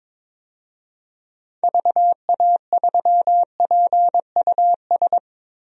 Morse code, heard as a keyed tone, about 22 wpm, 700 Hz. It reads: VA3PUS
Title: DXCC-100 Call Signs - 6 Characters - Repeated 2x - 22wpm